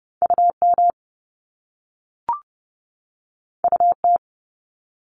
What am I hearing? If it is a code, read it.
Morse code, heard as a keyed tone, about 30 wpm, 700 Hz. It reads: UM UT
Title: Top Two Letter Combinations in Words 30wpm